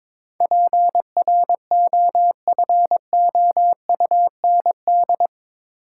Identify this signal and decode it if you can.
Morse code, heard as a keyed tone, about 22 wpm, 700 Hz. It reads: PROFOUND
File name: practice / 22wpm / Eight Letter Words - 22wpm